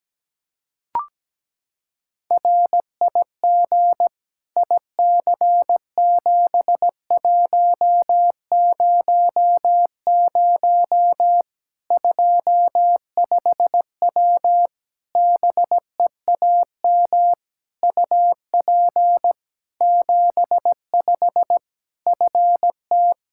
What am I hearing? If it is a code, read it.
Morse code, heard as a keyed tone, about 17 wpm, 700 Hz. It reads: RIG IC7100 25W BEAM UP 75 FT
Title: Practice QSO - Exchange 2 - Info - 17wpm